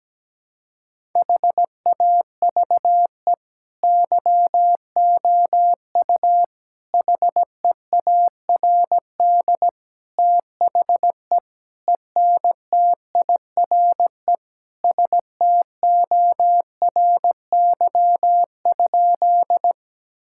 Morse code, heard as a keyed tone, about 17 wpm, 700 Hz. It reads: HAVE YOU HEARD THE ENTIRE STORY?
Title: Sentences from Top 1000 Words - Review Part 5 - 17wpm